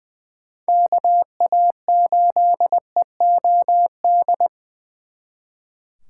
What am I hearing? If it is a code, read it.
Morse code, heard as a keyed tone, about 23 wpm, 700 Hz. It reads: KA8EOD
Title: Speed Racing - USA Call Signs 20wpm